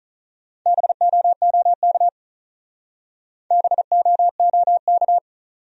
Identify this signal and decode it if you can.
Morse code, heard as a keyed tone, about 47 wpm, 700 Hz. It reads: BOOK BOOK